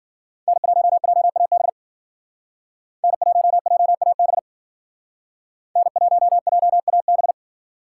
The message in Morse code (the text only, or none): N1JAB N1JAB N1JAB